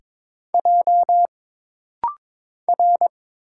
Morse code, heard as a keyed tone, about 22 wpm, 700 Hz. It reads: J R
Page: This rapid-fire timing is geared to push your ICR (Instant Character Recognition) skills.